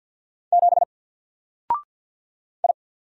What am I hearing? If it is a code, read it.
Morse code, heard as a keyed tone, about 50 wpm, 700 Hz. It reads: 7 I
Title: Single Letter-Number 50wpm